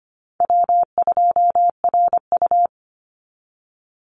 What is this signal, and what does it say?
Morse code, heard as a keyed tone, about 25 wpm, 700 Hz. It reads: W2RU
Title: Easy Call Signs Repeated 3x - 25wpm